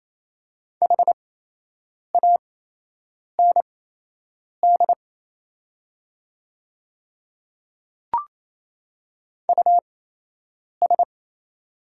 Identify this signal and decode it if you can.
Morse code, heard as a keyed tone, about 27 wpm, spaced as for 8 wpm, 700 Hz. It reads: HAND US
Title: 8x Character Spacing - Top 300 Words - 28wpm